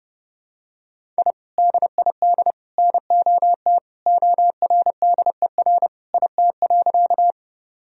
Morse code, heard as a keyed tone, about 32 wpm, 700 Hz. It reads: I DID NOT ORDER IT.
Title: Sentences from Top 1000 Words - Review Part 2 - 30wpm